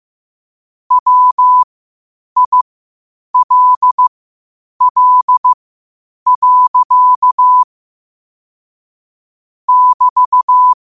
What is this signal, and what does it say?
Morse code, heard as a keyed tone, about 15 wpm, spaced as for 8 wpm, 1000 Hz. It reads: WILL. <BT>
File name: qotdfiles / 8wpm